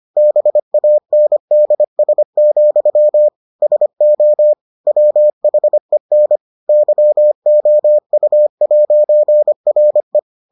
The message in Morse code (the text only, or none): BANDS, SO WHEN YOU'RE